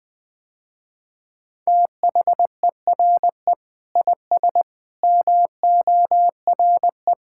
Morse code, heard as a keyed tone, about 20 wpm, 700 Hz. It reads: THERE IS MORE
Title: Sentences with 7-Word Phrases - Spoken Beginning - 20wpm